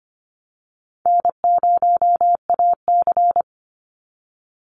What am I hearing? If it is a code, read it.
Morse code, heard as a keyed tone, about 25 wpm, 700 Hz. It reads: N0AC
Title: Easy Call Signs - 25wpm